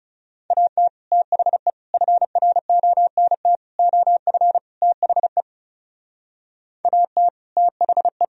Morse code, heard as a keyed tone, about 38 wpm, 700 Hz. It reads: AT THE FRONT OF THE AT THE
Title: Speed-Racing - Sets of 5 Words 30wpm